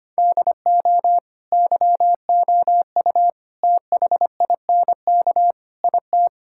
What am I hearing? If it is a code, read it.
Morse code, heard as a keyed tone, about 25 wpm, 700 Hz. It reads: DO YOU THINK IT
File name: practice / 25wpm / Sentences with 4-Word Phrases - 25wpm